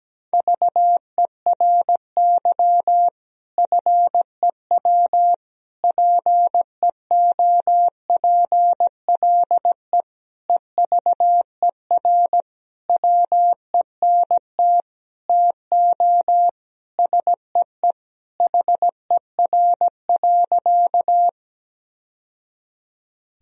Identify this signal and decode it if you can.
Morse code, heard as a keyed tone, about 17 wpm, 700 Hz. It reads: VERY FEW PEOPLE EVER WENT TO SEE HER.
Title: Sentences from Top 1000 Words - Review Part 1 - 17wpm